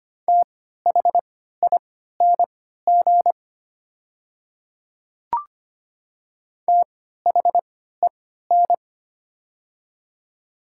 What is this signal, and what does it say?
Morse code, heard as a keyed tone, about 25 wpm, spaced as for 14 wpm, 700 Hz. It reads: THING THEN
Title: 3x Character Spacing - Top 300 Words - 25wpm